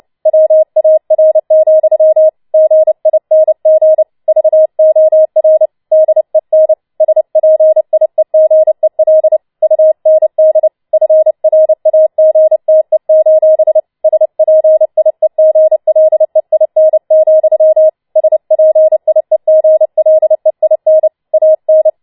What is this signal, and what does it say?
Morse code, heard as a keyed tone, about 29 wpm, 630 Hz. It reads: WAR, GING VOR DEN SPIEGEL UND FRAGTE: SPIEGLEIN, SPIEGLEIN AN